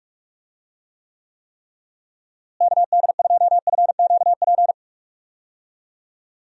Morse code, heard as a keyed tone, about 46 wpm, 700 Hz. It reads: KD2FQP
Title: USA Call Signs - 6 Characters - Repeated 3x - 45wpm